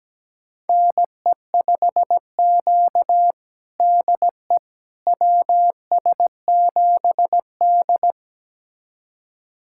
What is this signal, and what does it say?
Morse code, heard as a keyed tone, about 17 wpm, 700 Hz. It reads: NE5Q DE WS7D
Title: USA Callsign DE Callsign - 4 Characters - 17wpm